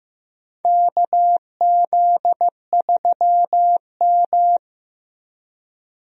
Morse code, heard as a keyed tone, about 15 wpm, 700 Hz. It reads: KZ3M
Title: USA Call Signs - 4 Characters - Repeated 2x - 15wpm